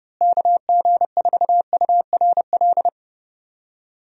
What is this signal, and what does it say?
Morse code, heard as a keyed tone, about 32 wpm, 700 Hz. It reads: KG4URL
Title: USA Call Signs - 6 Characters - Repeated 2x - 30wpm